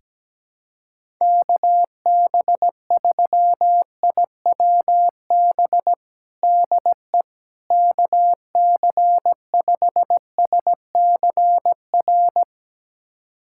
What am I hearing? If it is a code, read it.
Morse code, heard as a keyed tone, about 17 wpm, 700 Hz. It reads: KB3IWB DE KC5SCR
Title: USA Callsign DE Callsign - 6 Characters - 17wpm